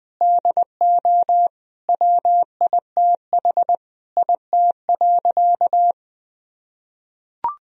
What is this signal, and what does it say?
Morse code, heard as a keyed tone, about 21 wpm, 700 Hz. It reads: DO WITH IT.
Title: Sentences from Top 1000 Words - Review Part 3 - 20wpm